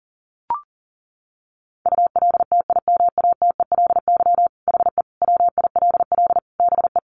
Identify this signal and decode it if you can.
Morse code, heard as a keyed tone, about 40 wpm, 700 Hz. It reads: ULTIMATELY HE WILL BE
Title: Sentences from Top 1200 Words 40wpm